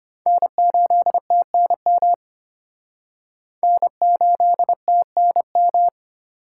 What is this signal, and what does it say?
Morse code, heard as a keyed tone, about 31 wpm, 700 Hz. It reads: N8TNM N8TNM